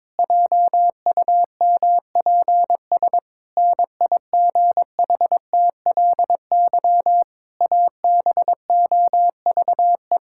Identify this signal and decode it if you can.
Morse code, heard as a keyed tone, about 22 wpm, 700 Hz. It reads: JUMPS NIGHTLY ABOVE
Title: Pangram Challenge v2 - 22wpm